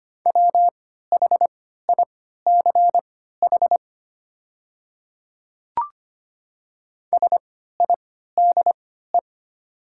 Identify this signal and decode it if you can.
Morse code, heard as a keyed tone, about 25 wpm, spaced as for 14 wpm, 700 Hz. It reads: WHICH SIDE
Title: 3x Character Spacing - Top 100 Words - 25wpm